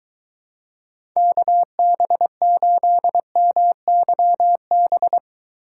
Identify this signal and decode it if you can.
Morse code, heard as a keyed tone, about 23 wpm, 700 Hz. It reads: KB8MYB